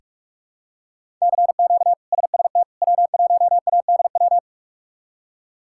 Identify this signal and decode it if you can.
Morse code, heard as a keyed tone, about 46 wpm, 700 Hz. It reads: CQ SST W1ADW
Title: K1USN SST QSO Practice - 45wpm